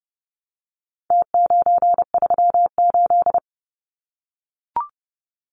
Morse code, heard as a keyed tone, about 30 wpm, 700 Hz. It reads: T938
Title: Numbers - Contest Style - 4-Digit 30wpm